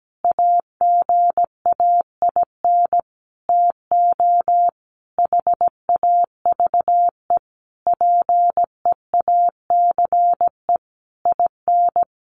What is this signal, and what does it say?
Morse code, heard as a keyed tone, about 17 wpm, 700 Hz. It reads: AGAIN TO HAVE PEACE IN
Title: Sentences from Top 1500 Words 17wpm